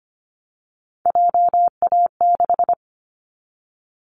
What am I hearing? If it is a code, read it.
Morse code, heard as a keyed tone, about 25 wpm, 700 Hz. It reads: JA6
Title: Callsign Prefixes for Top 25 DXCC - 25wpm